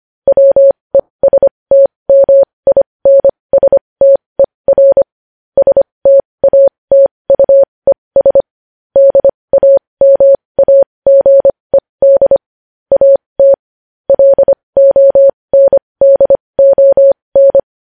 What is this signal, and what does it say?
Morse code, heard as a keyed tone, about 25 wpm, spaced as for 20 wpm, 550 Hz. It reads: WESTMINSTER STATUES DAMAGED AT LONDON